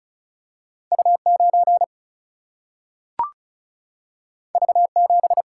Morse code, heard as a keyed tone, about 36 wpm, 700 Hz. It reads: U 9 V 7
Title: Single-Letter Single-Number 35wpm